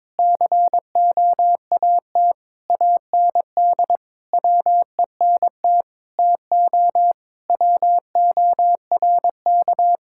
Morse code, heard as a keyed tone, about 22 wpm, 700 Hz. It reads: COAT AND WENT TO WORK
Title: Sentences from Top 1600 Words 22wpm